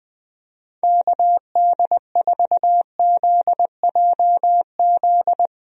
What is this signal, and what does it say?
Morse code, heard as a keyed tone, about 21 wpm, 700 Hz. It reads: KD4ZJZ
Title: USA Call Signs - 6 Characters - Repeated 2x - 20wpm